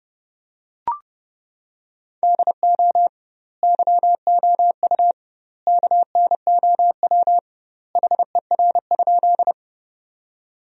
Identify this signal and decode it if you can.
Morse code, heard as a keyed tone, about 29 wpm, 700 Hz. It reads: DO YOU KNOW HER?
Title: 2x Word-Spacing - Sentences from Top 100 Words - 30wpm